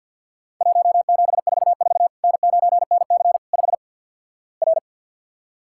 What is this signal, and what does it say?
Morse code, heard as a keyed tone, about 47 wpm, 700 Hz. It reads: 1744 N9NC 5 R
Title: Worked All Europe DX Contest - QTC Exchange - 50wpm